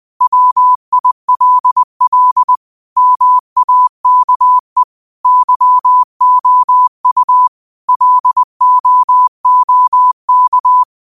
Morse code, heard as a keyed tone, about 20 wpm, 1000 Hz. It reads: WILL MAKE YOU LOOK
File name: qotdfiles / 20wpm